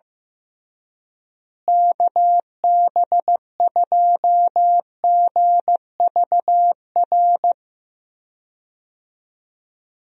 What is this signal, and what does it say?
Morse code, heard as a keyed tone, about 15 wpm, 700 Hz. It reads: KB2GVR
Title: USA Call Signs - 6 Characters - Repeated 3x - 15wpm